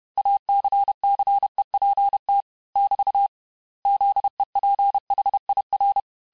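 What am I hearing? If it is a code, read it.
Morse code, heard as a keyed tone, about 31 wpm, 800 Hz. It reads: ACCEPT <BT> ZEPHIR
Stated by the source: Morse lesson of words, each word repeated 4 times, at 29 words per minute.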